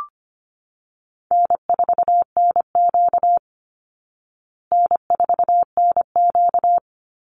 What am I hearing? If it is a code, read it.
Morse code, heard as a keyed tone, about 25 wpm, 700 Hz. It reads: N4NQ N4NQ
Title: Easy Call Signs Repeated 3x - 25wpm